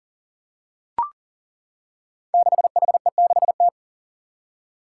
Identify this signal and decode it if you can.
Morse code, heard as a keyed tone, about 40 wpm, 700 Hz. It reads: 6HE6T
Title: Five Challenging Letter-Number Combinations 40wpm